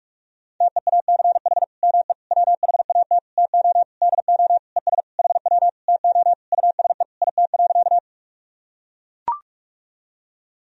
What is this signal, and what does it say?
Morse code, heard as a keyed tone, about 44 wpm, 700 Hz. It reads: TEACH ME WHAT TO DO ES HW TO USE IT.
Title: Sentences from Top 500 Words and Abbreviations 45wpm